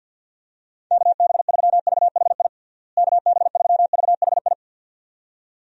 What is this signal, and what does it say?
Morse code, heard as a keyed tone, about 52 wpm, 700 Hz. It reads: KB3VHI KB3VHI
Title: USA Call Signs - 6 Characters - Repeated 2x - 50wpm